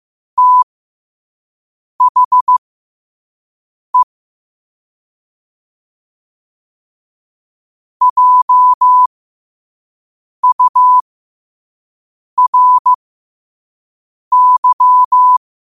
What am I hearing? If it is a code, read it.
Morse code, heard as a keyed tone, about 15 wpm, spaced as for 5 wpm, 1000 Hz. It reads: THE JURY
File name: qotdfiles / 5wpm